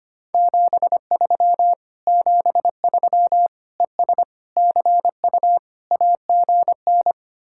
Morse code, heard as a keyed tone, about 30 wpm, 700 Hz. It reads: 73 73 ES CU AGN
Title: Speed-Racing Common QSO Phrases Encore - 25wpm